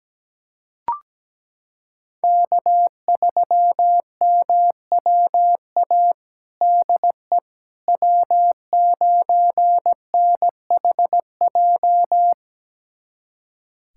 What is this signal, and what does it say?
Morse code, heard as a keyed tone, about 17 wpm, 700 Hz. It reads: K3MWA DE W9NHJ
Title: USA Callsign DE Callsign - 5 Characters - 17wpm